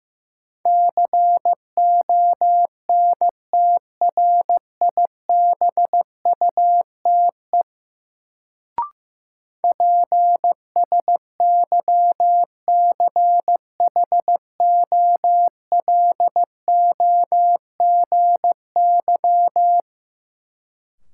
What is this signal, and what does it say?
Morse code, heard as a keyed tone, about 15 wpm, 700 Hz. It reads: CONTRIBUTE PSYCHOLOGY
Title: Sentences from Top 1400 Words - New Word Drill - 15wpm